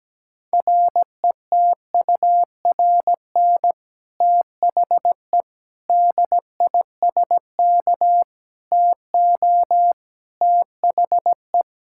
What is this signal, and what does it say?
Morse code, heard as a keyed tone, about 17 wpm, 700 Hz. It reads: RETURN THE DISK TO THE
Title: Sentences from Top 1000 Words - Review Part 6 - 17wpm